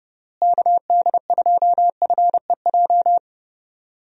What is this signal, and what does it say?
Morse code, heard as a keyed tone, about 30 wpm, 700 Hz. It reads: KD2FEJ
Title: USA Call Signs - 6 Characters - Repeated 2x - 30wpm